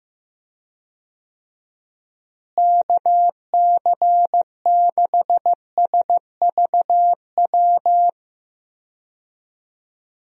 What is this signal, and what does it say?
Morse code, heard as a keyed tone, about 15 wpm, 700 Hz. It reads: KC6SVW
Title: USA Call Signs - 6 Characters - Repeated 2x - 15wpm